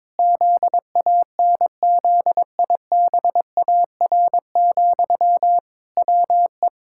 Morse code, heard as a keyed tone, about 22 wpm, 700 Hz. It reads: ZANZIBAR, WE
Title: Pangram Challenge v2 - 22wpm